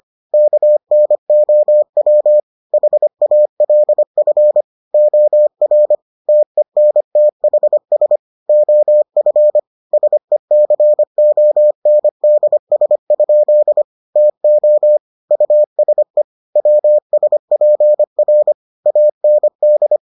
Morse code, heard as a keyed tone, about 25 wpm, 600 Hz. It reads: KNOW HALF OR TENTHS OF SECONDS? TO USE WSPR AND